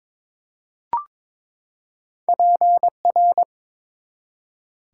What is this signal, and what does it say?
Morse code, heard as a keyed tone, about 22 wpm, 700 Hz. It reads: PR
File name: practice / 22wpm / ARRL Sections v2 - 22wpm